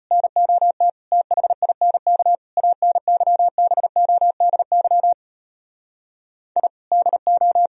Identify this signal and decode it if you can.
Morse code, heard as a keyed tone, about 33 wpm, 700 Hz. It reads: NOT THINK ANYBODY I DO